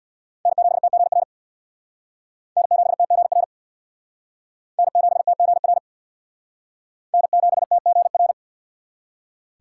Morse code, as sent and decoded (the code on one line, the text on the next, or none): -. --... - --. .-. / -. --... - --. .-. / -. --... - --. .-. / -. --... - --. .-.
N7TGR N7TGR N7TGR N7TGR